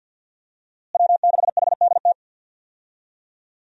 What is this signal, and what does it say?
Morse code, heard as a keyed tone, about 46 wpm, 700 Hz. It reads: W6HDT
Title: USA Call Signs - 5 Characters - Repeated 3x - 50wpm